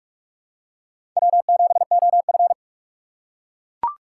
Morse code, heard as a keyed tone, about 42 wpm, 700 Hz. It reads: W7OF
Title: USA Call Signs - 4 Characters - 4x Character Spacing - 45wpm